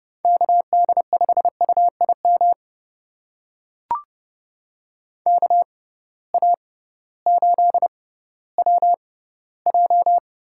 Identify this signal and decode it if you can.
Morse code, heard as a keyed tone, about 30 wpm, spaced as for 16 wpm, 700 Hz. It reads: KD5UIM KA8WJ
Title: USA Call Signs - 6 Characters - 6x Character Spacing - 30wpm